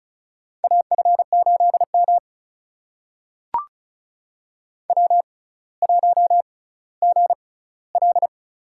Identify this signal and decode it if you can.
Morse code, heard as a keyed tone, about 36 wpm, spaced as for 18 wpm, 700 Hz. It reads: AF8M W1GL
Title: USA Call Signs - 4 Characters - 6x Character Spacing - 35wpm